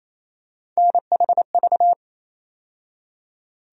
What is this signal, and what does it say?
Morse code, heard as a keyed tone, about 28 wpm, 700 Hz. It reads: NHV
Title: USA Call Sign Postfixes - 3 Letters 28wpm